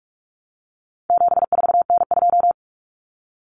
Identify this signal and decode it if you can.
Morse code, heard as a keyed tone, about 44 wpm, 700 Hz. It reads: 74N2
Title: Common Cut Numbers - 4-Digit 45wpm